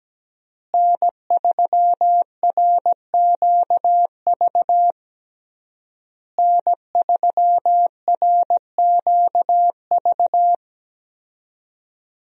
Morse code, heard as a keyed tone, about 17 wpm, 700 Hz. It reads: N3RQV N3RQV
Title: USA Call Signs - 5 Characters - Repeated 3x - 17wpm